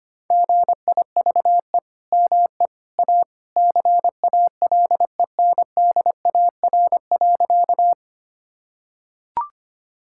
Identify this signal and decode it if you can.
Morse code, heard as a keyed tone, about 25 wpm, 700 Hz. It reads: GIVE ME A CALENDAR.
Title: Sentences from Top 1600 Words 25wpm